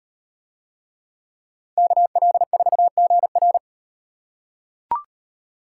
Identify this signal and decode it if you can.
Morse code, heard as a keyed tone, about 38 wpm, 700 Hz. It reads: KL4GR